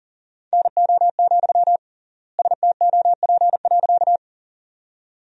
Morse code, heard as a keyed tone, about 40 wpm, 700 Hz. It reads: NO, STOP.
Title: 3x Word-Spacing - Sentences from Top 300 Words - 40wpm